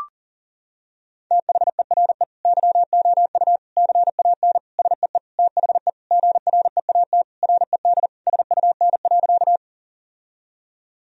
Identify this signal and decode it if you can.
Morse code, heard as a keyed tone, about 40 wpm, 700 Hz. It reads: THERE YOU CAN SEE THE GREAT RED SUN.
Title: Sentences from Top 1000 Words - Review Part 2 - 40wpm